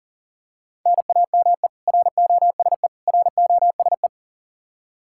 Morse code, heard as a keyed tone, about 40 wpm, 700 Hz. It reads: NAME ROSE ROSE
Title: Practice QSO - Exchange 1 - Beginning - 40wpm